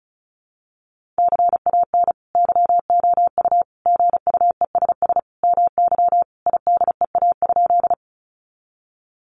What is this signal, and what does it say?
Morse code, heard as a keyed tone, about 34 wpm, 700 Hz. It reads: CAN YOU GUESS MY IDEA?
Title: Sentences from Top 1500 Words 35wpm